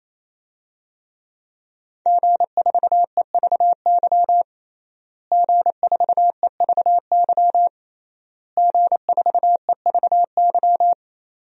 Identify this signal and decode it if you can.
Morse code, heard as a keyed tone, about 28 wpm, 700 Hz. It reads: G4EVY G4EVY G4EVY
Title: DXCC-100 Call Signs - 5 Characters - Repeated 3x - 28wpm